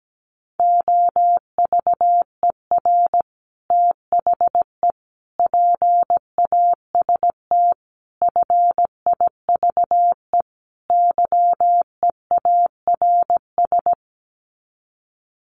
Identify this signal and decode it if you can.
Morse code, heard as a keyed tone, about 17 wpm, 700 Hz. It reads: OVER THE PAST FIVE YEARS
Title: Sets of 5 Words Encore 17wpm